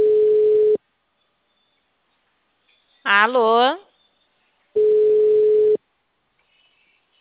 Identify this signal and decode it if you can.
Ringback tone (Europe, 425 Hz), with speech.